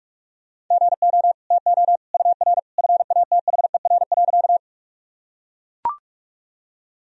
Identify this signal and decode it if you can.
Morse code, heard as a keyed tone, about 41 wpm, 700 Hz. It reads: GO TO UR FATHER.
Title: Sentences from Top 300 Words and Abbreviations 45wpm